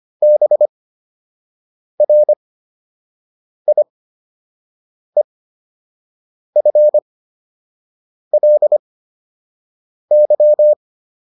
Morse code, heard as a keyed tone, about 25 wpm, spaced as for 6 wpm, 600 Hz. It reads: BRIEFLY